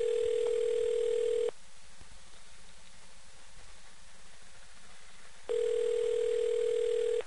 Ringback tone (North America), 440 + 480 Hz.